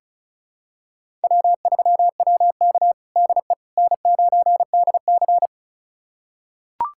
Morse code, heard as a keyed tone, about 36 wpm, 700 Hz. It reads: W3WK DE N9DC
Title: USA Callsign DE Callsign - 4 Characters - 35wpm